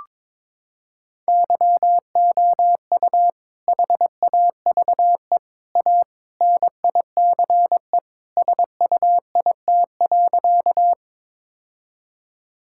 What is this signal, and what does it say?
Morse code, heard as a keyed tone, about 22 wpm, 700 Hz. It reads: YOU HAVE A NICE SUIT.
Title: Sentences from Top 1600 Words 22wpm